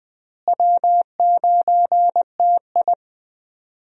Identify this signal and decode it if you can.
Morse code, heard as a keyed tone, about 21 wpm, 700 Hz. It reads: W9TI
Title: USA Call Signs - 4 Characters - Repeated 3x - 20wpm